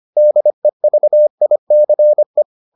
Morse code, heard as a keyed tone, about 25 wpm, 600 Hz. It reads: DEVICE